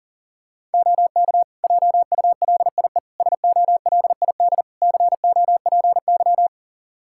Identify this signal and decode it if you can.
Morse code, heard as a keyed tone, about 40 wpm, 700 Hz. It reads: OK JULIE SOLID COPY
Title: Practice QSO - Exchange 3 - More Info - 40wpm